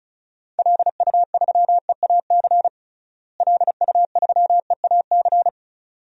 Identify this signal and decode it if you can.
Morse code, heard as a keyed tone, about 36 wpm, 700 Hz. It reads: LU3EAC LU3EAC
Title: DXCC-100 Call Signs - 6 Characters - Repeated 2x - 35wpm